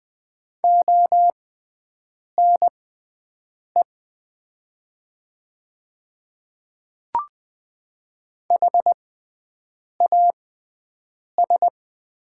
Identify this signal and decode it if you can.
Morse code, heard as a keyed tone, about 20 wpm, spaced as for 7 wpm, 700 Hz. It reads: ONE HAS
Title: 6x Character Spacing - Top 100 Words - 20wpm